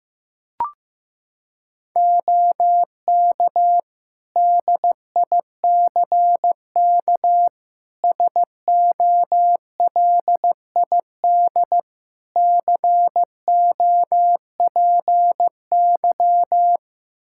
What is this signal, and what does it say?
Morse code, heard as a keyed tone, about 15 wpm, 700 Hz. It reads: OK DICK SOLID COPY
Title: Practice QSO - Exchange 3 - More Info - 15wpm